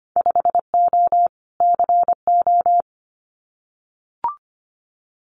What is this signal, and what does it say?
Morse code, heard as a keyed tone, about 25 wpm, 690 Hz. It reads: 5O CO
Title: Winter Field Day Exchanges Encore 25wpm